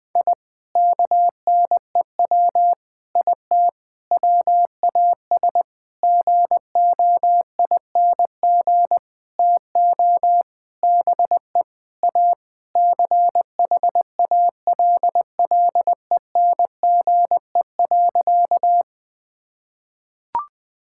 Morse code, heard as a keyed tone, about 20 wpm, 700 Hz. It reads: I KNEW IT WAS GOING TO BE A CHALLENGE.
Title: Sentences with 5-Word Phrases - 20wpm